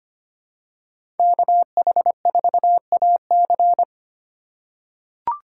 Morse code, heard as a keyed tone, about 25 wpm, 700 Hz. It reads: KH4AC
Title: USA Call Signs - 3x Character Spacing - 25wpm